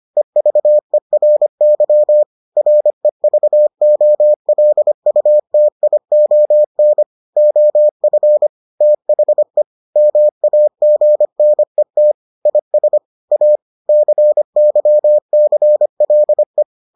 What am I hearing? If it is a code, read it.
Morse code, heard as a keyed tone, about 25 wpm, 600 Hz. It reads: EVERY REVOLUTION OF THE MAGNET IS A CYCLE